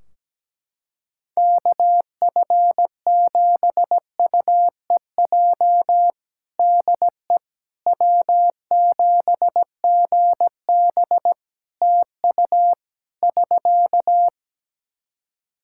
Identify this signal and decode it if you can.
Morse code, heard as a keyed tone, about 17 wpm, 700 Hz. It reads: KF7UEJ DE W7GB TU <SK>
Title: Practice QSO - Exchange 4 - Ending - 17wpm